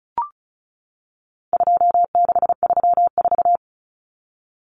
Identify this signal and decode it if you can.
Morse code, heard as a keyed tone, about 34 wpm, 700 Hz. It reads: 2634
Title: Numbers - 4-Digit 35wpm